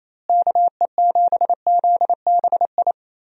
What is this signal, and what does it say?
Morse code, heard as a keyed tone, about 28 wpm, 700 Hz. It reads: KE7ZBI
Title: USA Call Signs - 6 Characters - Repeated 2x - 28wpm